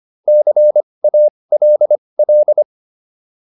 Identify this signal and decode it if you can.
Morse code, heard as a keyed tone, about 25 wpm, spaced as for 20 wpm, 600 Hz. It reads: CALL